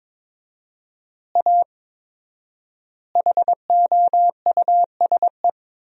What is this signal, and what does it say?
Morse code, heard as a keyed tone, about 23 wpm, 700 Hz. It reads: A HOUSE
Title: 4x Word-Spacing - Sentences from Top 200 Words - 22wpm